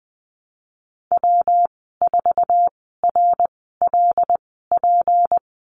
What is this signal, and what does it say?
Morse code, heard as a keyed tone, about 20 wpm, spaced as for 15 wpm, 700 Hz. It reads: W4RLP
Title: 193 Lesson 31g - Letter v - Call Signs 20wpm